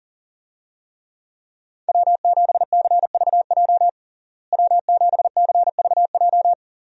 Morse code, heard as a keyed tone, about 40 wpm, 700 Hz. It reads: W7CVJ W7CVJ
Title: USA Call Signs - 5 Characters - Repeated 2x - 40wpm